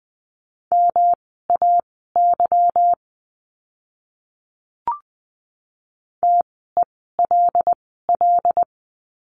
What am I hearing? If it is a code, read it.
Morse code, heard as a keyed tone, about 20 wpm, spaced as for 15 wpm, 700 Hz. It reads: MAY TELL
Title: 192 Lesson 31f - Letter v - Words 20wpm